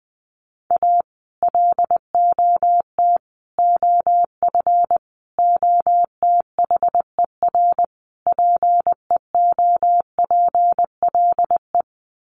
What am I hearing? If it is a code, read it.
Morse code, heard as a keyed tone, about 20 wpm, 700 Hz. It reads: A LOT OF OTHER PEOPLE
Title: Sets of 5 Words Encore2 20wpm